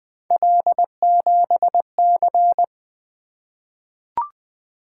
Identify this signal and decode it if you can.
Morse code, heard as a keyed tone, about 20 wpm, 700 Hz. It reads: L7C
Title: Three Letter-Number Combinations - 20wpm